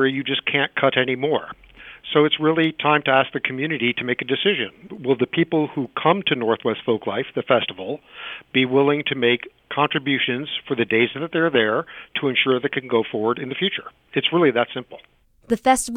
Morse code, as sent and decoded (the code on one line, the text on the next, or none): none